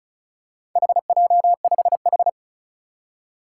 Morse code, heard as a keyed tone, about 36 wpm, 700 Hz. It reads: HJ5H
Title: Four Challenging Letter-Number Combinations 35wpm